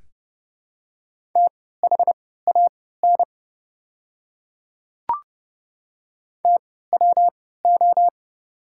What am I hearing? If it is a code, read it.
Morse code, heard as a keyed tone, about 31 wpm, spaced as for 17 wpm, 690 Hz. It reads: THAN TWO
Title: 3x Character Spacing - Top 500 Words - 30wpm